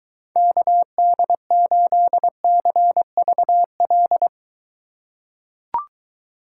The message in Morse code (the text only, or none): KD8CVL